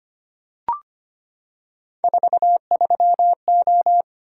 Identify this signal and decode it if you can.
Morse code, heard as a keyed tone, about 25 wpm, 700 Hz. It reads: 43O
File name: Three Letter-Number Combinations - 25wpm